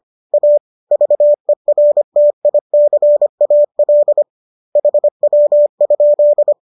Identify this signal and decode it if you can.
Morse code, heard as a keyed tone, about 25 wpm, 590 Hz. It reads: A VERTICAL HW?